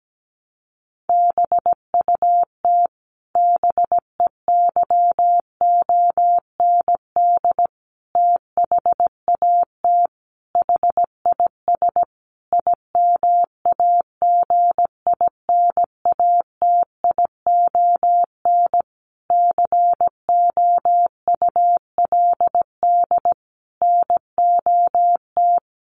Morse code, heard as a keyed tone, about 17 wpm, 700 Hz. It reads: BUT BEYOND THAT HIS IMAGINATION COULD NOT
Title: Sentences from Top 1300 Words 17wpm